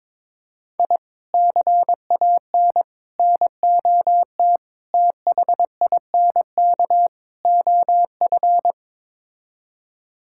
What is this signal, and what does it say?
Morse code, heard as a keyed tone, about 22 wpm, 700 Hz. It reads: I CAN NOT THINK OF
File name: practice / 22wpm / Build A Phrase - 5-Word Phrases - 22wpm